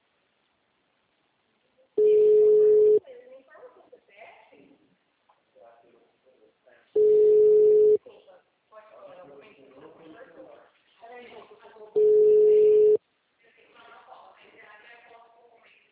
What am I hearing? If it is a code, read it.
Ringback tone (Europe, 425 Hz), with speech.